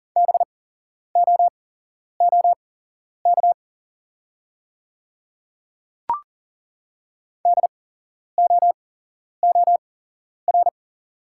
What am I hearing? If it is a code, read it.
Morse code, heard as a keyed tone, about 40 wpm, spaced as for 11 wpm, 700 Hz. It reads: BOOK DOOR
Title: 8x Character Spacing - Top 300 Words - 40wpm